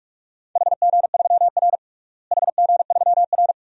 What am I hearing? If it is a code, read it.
Morse code, heard as a keyed tone, about 45 wpm, 700 Hz. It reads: HG3R HG3R
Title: DXCC-100 Call Signs - 4 Characters - Repeated 2x - 45wpm